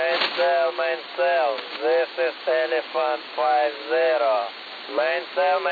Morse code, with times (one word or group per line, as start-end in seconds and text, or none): none